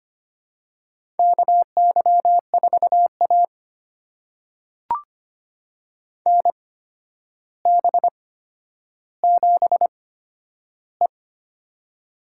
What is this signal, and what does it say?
Morse code, heard as a keyed tone, about 25 wpm, spaced as for 11 wpm, 700 Hz. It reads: KY4A NB7E
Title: USA Call Signs - 4 Characters - 8x Character Spacing - 25wpm